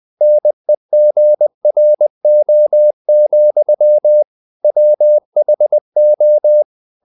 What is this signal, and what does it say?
Morse code, heard as a keyed tone, about 20 wpm, 600 Hz. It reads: NEGRO, WHO